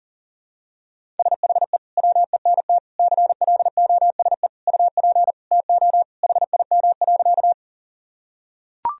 Morse code, heard as a keyed tone, about 37 wpm, 700 Hz. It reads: SHE WENT CLOSE UP TO HIM.
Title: Sentences from Top 1000 Words - Review Part 1 - 40wpm